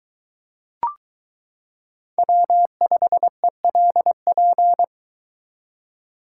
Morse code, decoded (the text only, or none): W5ELP